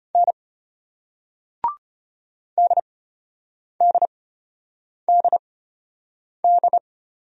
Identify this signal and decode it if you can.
Morse code, heard as a keyed tone, about 33 wpm, spaced as for 8 wpm, 690 Hz. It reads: N DDDD